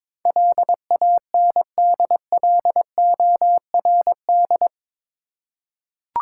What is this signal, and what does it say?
Morse code, heard as a keyed tone, about 22 wpm, 700 Hz. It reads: LANDLORD
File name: practice / 22wpm / Eight Letter Words - 22wpm